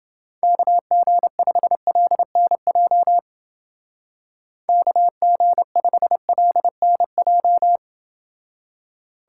Morse code, heard as a keyed tone, about 29 wpm, 700 Hz. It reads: KG5LNJ KG5LNJ